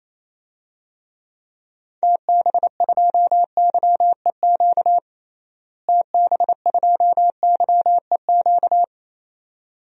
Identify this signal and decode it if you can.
Morse code, heard as a keyed tone, about 27 wpm, 700 Hz. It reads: TB2YEQ TB2YEQ
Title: DXCC-100 Call Signs - 6 Characters - Repeated 2x - 28wpm